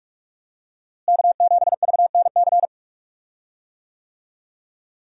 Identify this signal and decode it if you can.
Morse code, heard as a keyed tone, about 46 wpm, 700 Hz. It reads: K7VNC
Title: USA Call Signs - 5 Characters - Repeated 3x - 45wpm